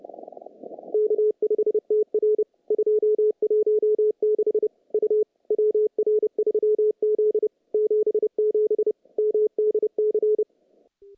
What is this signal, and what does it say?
Morse code, heard as a keyed tone, about 30 wpm, 410 Hz. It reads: K5TR 216 U WR3Z 77 MDC
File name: contests / CW-SS-k5tr- -0000